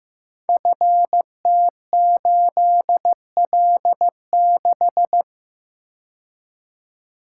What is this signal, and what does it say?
Morse code, heard as a keyed tone, about 15 wpm, 700 Hz. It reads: FT8L6
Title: Code Groups - 5 Letter:Numbers - 15wpm